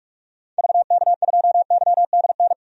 Morse code, heard as a keyed tone, about 46 wpm, 700 Hz. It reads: VK2YDN
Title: DXCC-100 Call Signs - 6 Characters - 45wpm